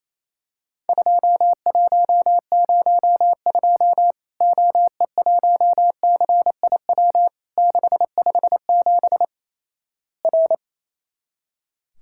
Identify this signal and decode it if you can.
Morse code, heard as a keyed tone, about 28 wpm, 700 Hz. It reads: 2102 OE1CIW 657 R
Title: Worked All Europe DX Contest - QTC Exchange - 28wpm